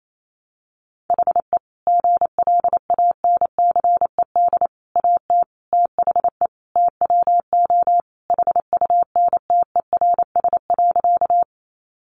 Morse code, heard as a keyed tone, about 28 wpm, 700 Hz. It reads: HE GLANCED AT THE TWO HUNTERS.
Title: A Sound Of Thunder 28wpm